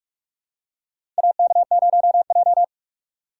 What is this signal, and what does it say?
Morse code, heard as a keyed tone, about 44 wpm, 700 Hz. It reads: AK0J
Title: USA Call Signs - 4 Characters - 4x Character Spacing - 45wpm